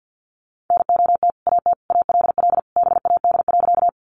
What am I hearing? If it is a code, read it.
Morse code, heard as a keyed tone, about 50 wpm, 700 Hz. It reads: NOT AT ALL BAD.
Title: Sentences from Top 700 Words 50wpm